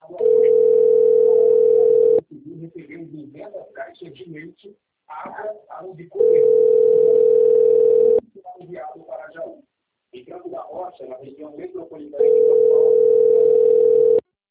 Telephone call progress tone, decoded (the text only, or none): ringback tone